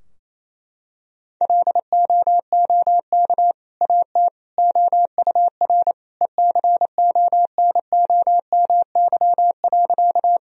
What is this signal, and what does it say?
Morse code, heard as a keyed tone, about 28 wpm, 700 Hz. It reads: LOOK AT OUR ECONOMY.
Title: Sentences from Top 1000 Words - Review Part 3 - 28wpm